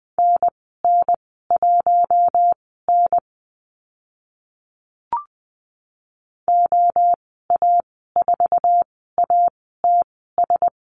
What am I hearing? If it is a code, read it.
Morse code, heard as a keyed tone, about 20 wpm, spaced as for 15 wpm, 700 Hz. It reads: NN1N OA4ATS
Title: 082 Lesson 14g - Number 2 - Call Signs 20wpm